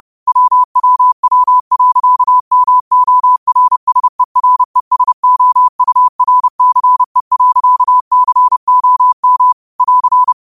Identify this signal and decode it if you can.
Morse code, heard as a keyed tone, about 32 wpm, 1000 Hz. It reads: WWW.MORSERESOURCE.COM <AR>
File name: qotdfiles / 30wpm